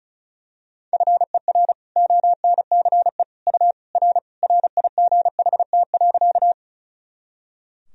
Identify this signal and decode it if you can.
Morse code, heard as a keyed tone, about 34 wpm, 700 Hz. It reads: FER ONCE U R RIGHT.
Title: Sentences from Top 300 Words and Abbreviations 35wpm